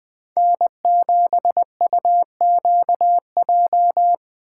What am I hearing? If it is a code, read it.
Morse code, heard as a keyed tone, about 20 wpm, 700 Hz. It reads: N7UQJ